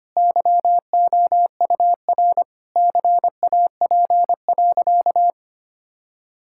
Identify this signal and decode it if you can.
Morse code, heard as a keyed tone, about 25 wpm, 700 Hz. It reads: YOUR CAP.
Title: Sentences from Top 1600 Words 25wpm